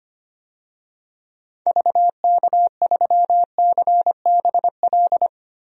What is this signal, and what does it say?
Morse code, heard as a keyed tone, about 25 wpm, 700 Hz. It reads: VK3CBL
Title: DXCC-100 Call Signs - 6 Characters - Repeated 2x - 25wpm